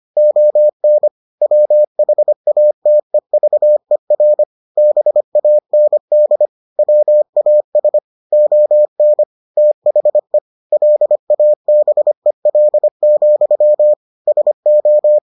Morse code, heard as a keyed tone, about 25 wpm, 600 Hz. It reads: ON WHATEVER BAND WAS ON THE LABEL, SO